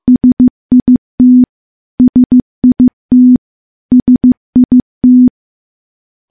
Morse code, heard as a keyed tone, about 15 wpm, 250 Hz. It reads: SIT SIT SIT
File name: morse / morse-training-15WPM